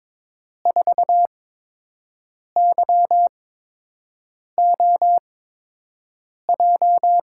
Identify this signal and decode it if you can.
Morse code, heard as a keyed tone, about 22 wpm, spaced as for 6 wpm, 700 Hz. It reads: 4YOJ
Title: USA Call Signs - 8x Character Spacing - 22wpm